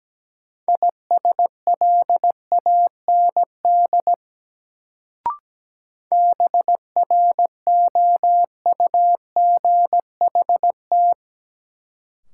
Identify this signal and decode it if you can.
Morse code, heard as a keyed tone, about 17 wpm, 700 Hz. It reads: ISLAND BROUGHT
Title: Sentences from Top 500 Words - New Word Drill - 17wpm